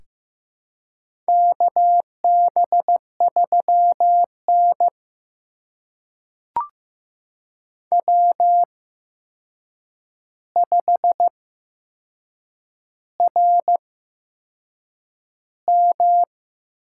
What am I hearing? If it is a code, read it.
Morse code, heard as a keyed tone, about 15 wpm, spaced as for 6 wpm, 700 Hz. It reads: KB3N W5RM
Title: USA Call Signs - 4 Characters - 8x Character Spacing - 15wpm